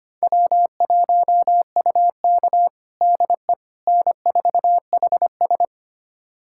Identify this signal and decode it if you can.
Morse code, heard as a keyed tone, about 25 wpm, 700 Hz. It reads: W1UK DE N4HS
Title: USA Callsign DE Callsign - 4 Characters - 25wpm